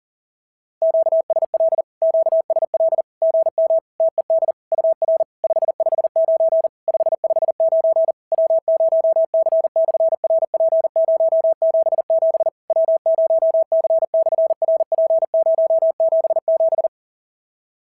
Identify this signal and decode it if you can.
Morse code, heard as a keyed tone, about 38 wpm, 690 Hz. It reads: QSL QSL GM TED UR 559 559 W0C/RP077 W0C/RP077
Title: SOTA QSO Practice 40wpm